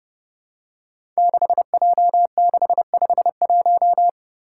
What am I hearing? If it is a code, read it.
Morse code, heard as a keyed tone, about 30 wpm, 700 Hz. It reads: 6J651
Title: Five Challenging Letter-Number Combinations 30wpm